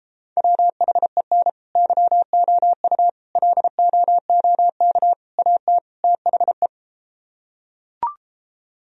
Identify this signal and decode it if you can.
Morse code, heard as a keyed tone, about 33 wpm, 700 Hz. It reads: WHEN YOU LOOK AT THE